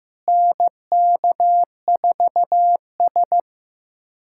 Morse code, heard as a keyed tone, about 15 wpm, 700 Hz. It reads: NK4S
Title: USA Callsign DE Callsign 15wpm